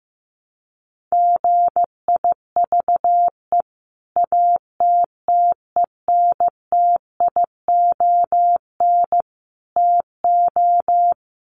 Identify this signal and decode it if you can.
Morse code, heard as a keyed tone, about 15 wpm, 700 Hz. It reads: GIVE ATTENTION TO
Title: Sentences from Top 1000 Words 15wpm